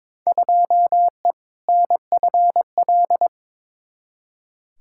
Morse code, heard as a keyed tone, about 22 wpm, 700 Hz. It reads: 2E NFL
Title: Field Day Exchanges Encore - 22wpm